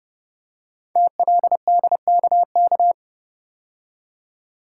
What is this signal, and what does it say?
Morse code, heard as a keyed tone, about 29 wpm, 700 Hz. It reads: TLDKK
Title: Code Groups - 5 Letters - 30wpm